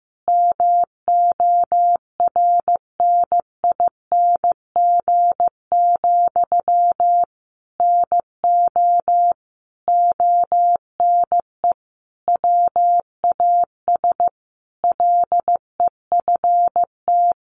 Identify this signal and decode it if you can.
Morse code, heard as a keyed tone, about 15 wpm, 700 Hz. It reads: MORNING, NO ONE WAS LEFT
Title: Sentences from Top 1400 Words 15wpm